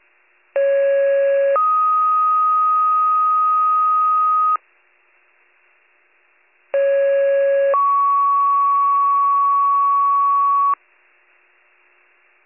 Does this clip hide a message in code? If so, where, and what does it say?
Yes: two-tone page from 0.6 to 10.7 s, heard as a one-second tone, then a three-second tone, about 550 then 1250 Hz, about 550 then 1120 Hz.